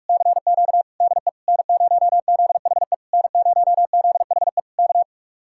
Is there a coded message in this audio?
Yes: Morse code from 0.1 to 5.0 s, heard as a keyed tone, about 59 wpm, 700 Hz. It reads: CQ DE N0ZHE N0ZHE K